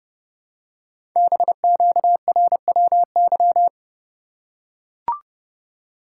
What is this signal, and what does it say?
Morse code, heard as a keyed tone, about 30 wpm, 700 Hz. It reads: BQRWY
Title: Code Groups - 5 Letters - 30wpm